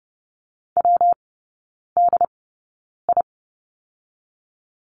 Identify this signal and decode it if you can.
Morse code, heard as a keyed tone, about 30 wpm, spaced as for 9 wpm, 700 Hz. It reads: WDI
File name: Lesson 19e - Letter w - Three Characters 30wpm